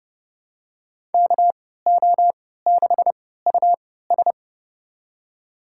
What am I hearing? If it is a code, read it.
Morse code, heard as a keyed tone, about 30 wpm, spaced as for 17 wpm, 700 Hz. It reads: KO6US
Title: USA Call Signs - 5 Characters - 3x Character Spacing - 30wpm